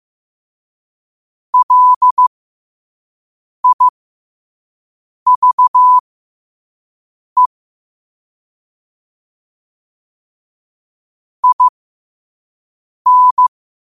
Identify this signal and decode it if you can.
Morse code, heard as a keyed tone, about 15 wpm, spaced as for 5 wpm, 1000 Hz. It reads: LIVE IN
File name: qotdfiles / 5wpm